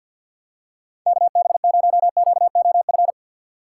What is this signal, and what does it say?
Morse code, heard as a keyed tone, about 53 wpm, 700 Hz. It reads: KB0QOF
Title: USA Call Signs - 6 Characters - 8x Character Spacing - 50wpm